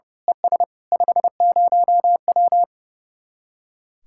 Morse code, heard as a keyed tone, about 30 wpm, 700 Hz. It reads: ES 50W
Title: Speed-Racing Common QSO Phrases - 30wpm